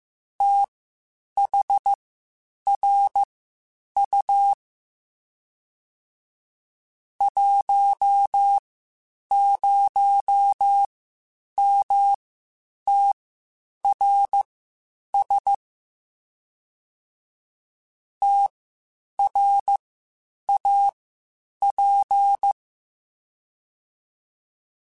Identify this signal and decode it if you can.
Morse code, heard as a keyed tone, about 15 wpm, spaced as for 8 wpm, 800 Hz. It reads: THRU 10MTRS TRAP
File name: QSOs / QSOs-mp3s / WD0FBT 1@13